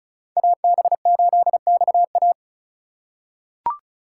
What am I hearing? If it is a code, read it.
Morse code, heard as a keyed tone, about 36 wpm, 700 Hz. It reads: AB8XA
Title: USA Call Signs - 8x Character Spacing - 35wpm